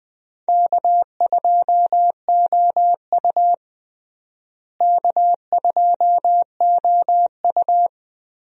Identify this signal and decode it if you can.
Morse code, heard as a keyed tone, about 20 wpm, 700 Hz. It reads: K2OU K2OU
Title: USA Call Signs - 4 Characters - Repeated 3x - 20wpm